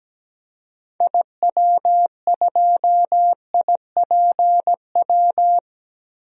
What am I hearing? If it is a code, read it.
Morse code, heard as a keyed tone, about 17 wpm, 700 Hz. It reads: IW2IPW
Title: DXCC-100 Call Signs - 6 Characters - 17wpm